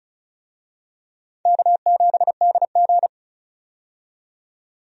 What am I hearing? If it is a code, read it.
Morse code, heard as a keyed tone, about 34 wpm, 700 Hz. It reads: K7DG
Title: USA Call Signs - 4 Characters - 35wpm